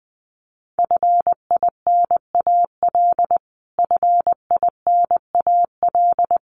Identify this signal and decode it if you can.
Morse code, heard as a keyed tone, about 20 wpm, 700 Hz. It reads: FINAL FINAL
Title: Top 500 Words Repeated 20wpm